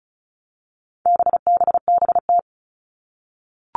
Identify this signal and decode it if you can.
Morse code, heard as a keyed tone, about 34 wpm, 700 Hz. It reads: BBBT
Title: All Cut Numbers - 4-Digit 35wpm